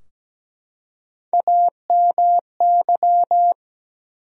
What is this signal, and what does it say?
Morse code, heard as a keyed tone, about 17 wpm, 700 Hz. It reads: AMY
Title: USA Common Names - Max 4 Letters - 17wpm